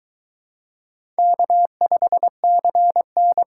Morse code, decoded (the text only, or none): K5CN